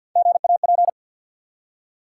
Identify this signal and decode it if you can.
Morse code, heard as a keyed tone, about 48 wpm, 700 Hz. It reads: GAP
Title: Three Letter Words - 50wpm